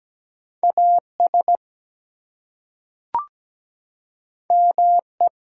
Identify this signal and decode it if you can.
Morse code, heard as a keyed tone, about 17 wpm, 700 Hz. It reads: AS ME
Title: Two Letter Words - 17wpm